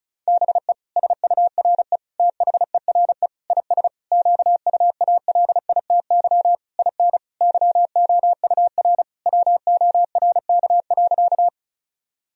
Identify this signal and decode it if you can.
Morse code, heard as a keyed tone, about 36 wpm, 700 Hz. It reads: BE SURE THERE IS QUALITY IN YOUR WORK.
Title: Sentences from Top 1000 Words - Review Part 4 - 35wpm